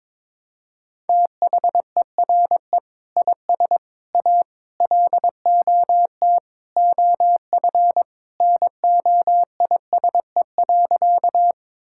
Morse code, heard as a keyed tone, about 22 wpm, 700 Hz. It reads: THERE IS A LOT OF NOISE.
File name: practice / 22wpm / Sentences with 2-Word Phrases - 22wpm